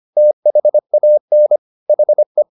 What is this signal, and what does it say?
Morse code, heard as a keyed tone, about 25 wpm, 600 Hz. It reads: THAN HE
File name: the old man and the sea part 1 25wpm